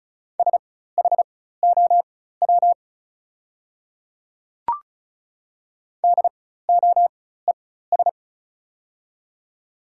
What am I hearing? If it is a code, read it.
Morse code, heard as a keyed tone, about 34 wpm, spaced as for 16 wpm, 700 Hz. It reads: SHOW DOES
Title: 4x Character Spacing - Top 200 Words - 35wpm